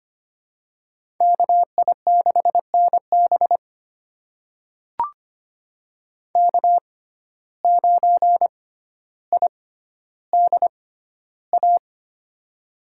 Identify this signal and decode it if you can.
Morse code, heard as a keyed tone, about 25 wpm, spaced as for 13 wpm, 700 Hz. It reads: KI6NB K9IDA
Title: USA Call Signs - 5 Characters - 6x Character Spacing - 25wpm